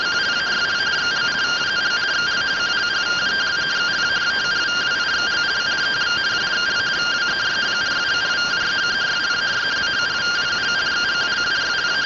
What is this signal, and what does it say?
Two-tone teleprinter signal, about 170 Hz shift.
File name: findings / CT AXIAL -256.gen-14 rtty rx